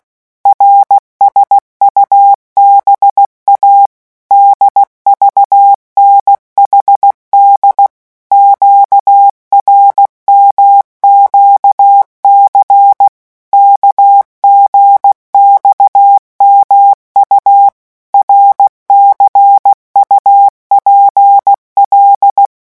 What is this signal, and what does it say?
Morse code, heard as a keyed tone, about 16 wpm, 800 Hz. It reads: RSUBA DVNHD QRMQC KGXMU RCUPL
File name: morse mp3 / revision5-vitesse 15 mots